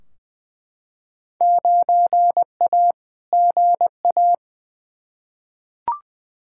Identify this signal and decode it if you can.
Morse code, heard as a keyed tone, about 21 wpm, 700 Hz. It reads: 9A GA
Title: Field Day Exchanges Encore - 20wpm